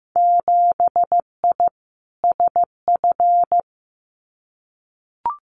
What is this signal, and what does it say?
Morse code, heard as a keyed tone, about 15 wpm, 700 Hz. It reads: 7I SF
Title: Winter Field Day Exchanges Encore 15wpm